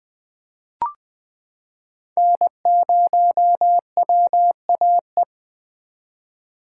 Morse code, heard as a keyed tone, about 20 wpm, 700 Hz. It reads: N0WAE
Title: 2023 SOTA Chaser Call Signs - 20wpm